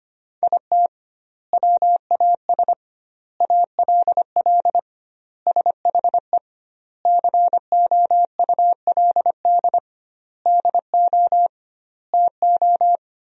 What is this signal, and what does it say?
Morse code, heard as a keyed tone, about 25 wpm, 700 Hz. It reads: IT WAS ALL SHE COULD DO TO
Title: 2x Word-Spacing - Sentences from Top 400 Words - 25wpm